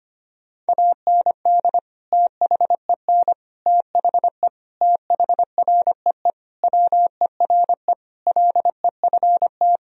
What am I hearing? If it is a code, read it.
Morse code, heard as a keyed tone, about 25 wpm, 700 Hz. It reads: AND THEN THE THREE WERE LEFT
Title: Sentences from Top 1000 Words - Review Part 1 - 25wpm